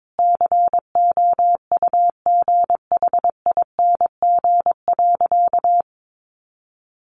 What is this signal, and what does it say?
Morse code, heard as a keyed tone, about 22 wpm, 700 Hz. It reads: COUGHING.
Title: Additional Easy Phrases 22wpm